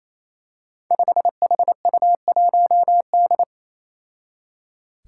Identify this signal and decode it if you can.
Morse code, heard as a keyed tone, about 28 wpm, 700 Hz. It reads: 5HU1D
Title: Five Challenging Letter-Number Combinations 28wpm